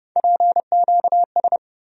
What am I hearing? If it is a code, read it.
Morse code, heard as a keyed tone, about 32 wpm, 700 Hz. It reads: PQS
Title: USA Call Sign Postfixes - 3 Letters 30wpm